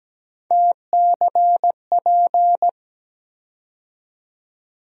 Morse code, heard as a keyed tone, about 17 wpm, 700 Hz. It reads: TCP
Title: USA Call Sign Postfixes - 3 Letters 17wpm